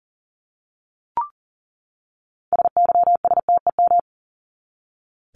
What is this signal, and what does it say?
Morse code, heard as a keyed tone, about 40 wpm, 700 Hz. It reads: SYSTEM
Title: Top 501-750 Words 40wpm